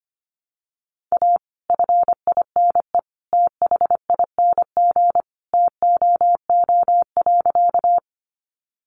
Morse code, heard as a keyed tone, about 25 wpm, 700 Hz. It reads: A FINE THING TOO.
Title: Sentences from Top 500 Words 25wpm